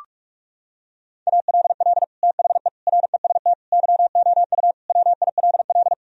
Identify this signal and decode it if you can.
Morse code, heard as a keyed tone, about 46 wpm, 700 Hz. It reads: ALL THE REST YOU WILL
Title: Sentences from Top 1000 Words - Review Part 4 - 45wpm